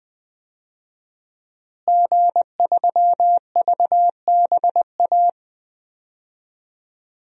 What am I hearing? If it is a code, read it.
Morse code, heard as a keyed tone, about 20 wpm, 700 Hz. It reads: G3VBA
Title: DXCC-100 Call Signs - 5 Characters - Repeated 3x - 20wpm